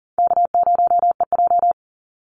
Morse code, heard as a keyed tone, about 40 wpm, 700 Hz. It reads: K0EJ
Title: Easy Call Signs - 40wpm